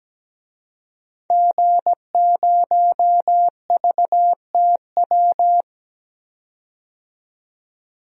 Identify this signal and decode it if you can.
Morse code, heard as a keyed tone, about 17 wpm, 700 Hz. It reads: G0VTW
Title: DXCC-100 Call Signs - 5 Characters - 17wpm